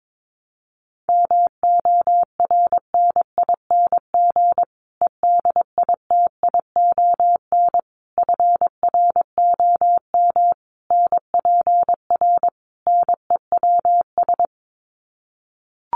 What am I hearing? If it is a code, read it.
Morse code, heard as a keyed tone, about 22 wpm, 700 Hz. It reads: MORNING EDITION FROM NPR NEWS
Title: Sets of 5 Words Encore 22wpm